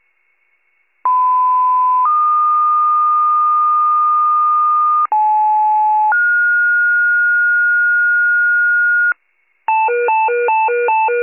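Two-tone page, heard as a one-second tone, then a three-second tone, about 1010 then 1220 Hz, about 860 then 1430 Hz, with noise. It is heard from 1.0 to 9.1 s.
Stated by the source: County-wide rescue paging alert.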